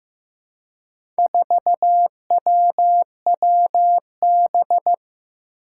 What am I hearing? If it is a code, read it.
Morse code, heard as a keyed tone, about 15 wpm, 700 Hz. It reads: 4WWB
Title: Four Challenging Letter-Number Combinations 15wpm